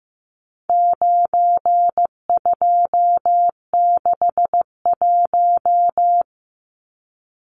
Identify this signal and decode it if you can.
Morse code, heard as a keyed tone, about 15 wpm, 700 Hz. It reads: 9261
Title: Numbers - 4-Digit 15wpm